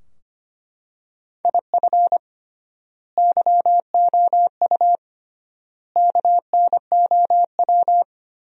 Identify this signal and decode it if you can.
Morse code, heard as a keyed tone, about 25 wpm, 700 Hz. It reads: IF YOU KNOW
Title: 3x Word-Spacing - Sentences from Top 400 Words - 25wpm